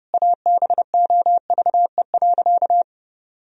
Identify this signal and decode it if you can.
Morse code, heard as a keyed tone, about 30 wpm, 700 Hz. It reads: ABOVE.
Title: Sentences from Top 1000 Words - Review Part 2 - 30wpm